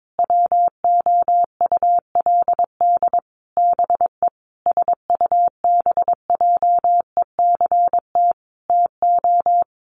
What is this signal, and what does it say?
Morse code, heard as a keyed tone, about 22 wpm, 700 Hz. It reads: WOULD BE SUBJECT TO
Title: Sentences from Top 1300 Words 22wpm